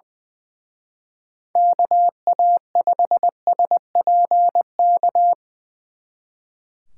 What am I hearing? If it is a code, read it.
Morse code, heard as a keyed tone, about 24 wpm, 700 Hz. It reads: KA5SPK
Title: Speed Racing - USA 2x3 Call Signs 20wpm